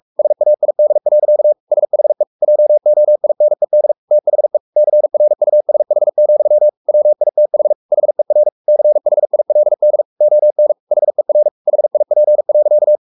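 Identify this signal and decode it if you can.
Morse code, heard as a keyed tone, about 44 wpm, 600 Hz. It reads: SAID. SHE JOINED THE CRUSH, WITH HER CHILD ON HER HIP.
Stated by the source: Morse code at 43 wpm sending Ken Follett's novel Never in English, from CW med Gnister.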